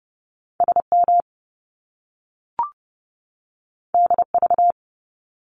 Morse code, heard as a keyed tone, about 30 wpm, 700 Hz. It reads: SM DV
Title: Top Two Letter Combinations in Words 30wpm